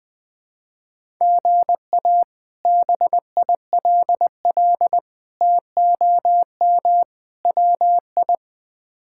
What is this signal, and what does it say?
Morse code, heard as a keyed tone, about 20 wpm, 700 Hz. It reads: GA BILL TOM WI
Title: K1USN SST QSO Practice - 20wpm